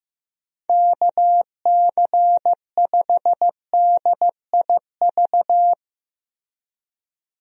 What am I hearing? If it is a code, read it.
Morse code, heard as a keyed tone, about 15 wpm, 700 Hz. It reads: KC5DIV
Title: USA Call Signs - 6 Characters - Repeated 2x - 15wpm